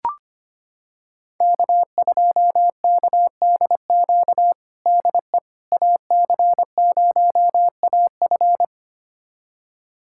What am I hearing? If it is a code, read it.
Morse code, heard as a keyed tone, about 25 wpm, 700 Hz. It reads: K2KDQ DE AC0AF
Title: USA Callsign DE Callsign - 5 Characters - 25wpm